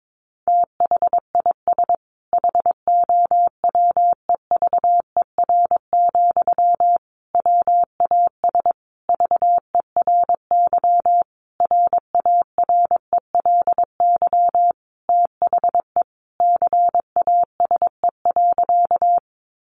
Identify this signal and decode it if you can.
Morse code, heard as a keyed tone, about 22 wpm, 700 Hz. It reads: THIS HOWEVER, WAS VERY RARELY THE CASE.
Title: Sentences from Top 1400 Words 22wpm